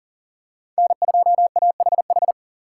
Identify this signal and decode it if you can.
Morse code, heard as a keyed tone, about 39 wpm, 700 Hz. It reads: N2AHH
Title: USA Call Signs - 5 Characters - 3x Character Spacing - 40wpm